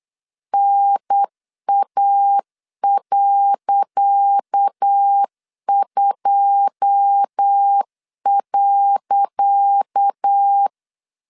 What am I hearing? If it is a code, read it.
Morse code, heard as a keyed tone, about 8 wpm, 800 Hz. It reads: NA.2.
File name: LEZIONI / 10 SIGNS / 10C DOT 8wpm 3649